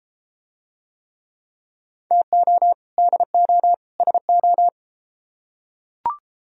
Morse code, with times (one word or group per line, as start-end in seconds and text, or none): 2.1-2.7 TO
3.0-3.7 DO
4.0-4.7 SO